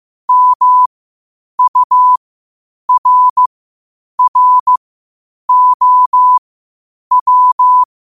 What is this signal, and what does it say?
Morse code, heard as a keyed tone, about 15 wpm, spaced as for 8 wpm, 1000 Hz. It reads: MURROW
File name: qotdfiles / 8wpm